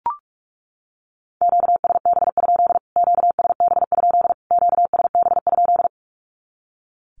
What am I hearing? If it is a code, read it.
Morse code, heard as a keyed tone, about 44 wpm, 700 Hz. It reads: QSB? QSB? QSB?
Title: Q-Codes Repeated 3x 45wpm